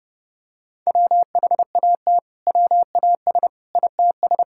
Morse code, heard as a keyed tone, about 32 wpm, 700 Hz. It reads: WHAT WAS ITS
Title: Sentences from Top 1000 Words - Review Part 5 - 30wpm